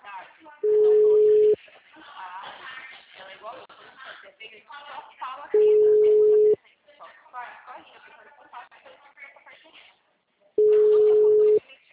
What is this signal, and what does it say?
Ringback tone (Europe, 425 Hz), with speech.
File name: out- -2011- - - .37647